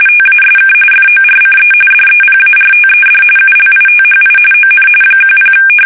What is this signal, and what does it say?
RTTY, about 45.45 baud. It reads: TASK FORCE DEPARTED ULITHI ON 10
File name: fsk850 / USS-Massachusetts